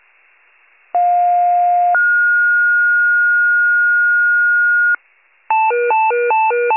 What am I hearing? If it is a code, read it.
Two-tone page, heard as a one-second tone, then a three-second tone, about 710 then 1430 Hz. It is heard from 0.9 to 5.0 s.